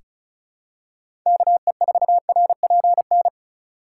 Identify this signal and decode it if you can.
Morse code, heard as a keyed tone, about 36 wpm, 700 Hz. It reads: KE4RPN
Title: USA Call Signs - 3x Character Spacing - 35wpm